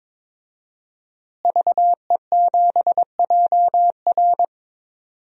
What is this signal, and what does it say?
Morse code, heard as a keyed tone, about 22 wpm, 700 Hz. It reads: VE7JR
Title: DXCC-100 Call Signs - 5 Characters - Repeated 2x - 22wpm